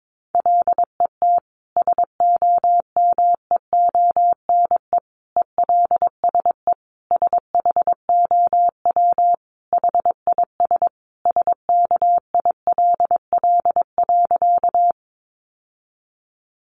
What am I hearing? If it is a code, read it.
Morse code, heard as a keyed tone, about 22 wpm, 700 Hz. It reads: LET SOMEONE ELSE SHOW HIS SKILL.
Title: Sentences from Top 1200 Words 22wpm